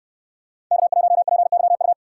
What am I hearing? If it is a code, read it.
Morse code, heard as a keyed tone, about 65 wpm, 700 Hz. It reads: K1PYU